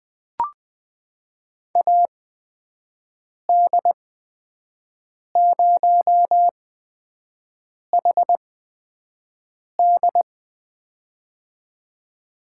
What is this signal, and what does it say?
Morse code, heard as a keyed tone, about 20 wpm, spaced as for 5 wpm, 700 Hz. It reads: AD0HD
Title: USA Call Signs - 8x Character Spacing - 20wpm